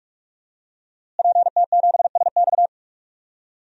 Morse code, heard as a keyed tone, about 42 wpm, 700 Hz. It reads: PT7SX
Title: DXCC-100 Call Signs - 5 Characters - Repeated 2x - 45wpm